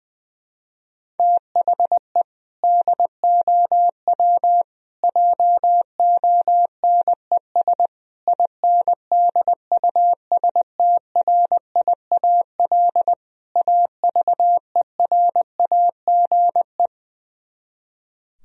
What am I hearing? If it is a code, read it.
Morse code, heard as a keyed tone, about 20 wpm, 700 Hz. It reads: THE DOW JONES INDUSTRIAL AVERAGE
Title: Speed-Racing - Sets of 5 Words 20wpm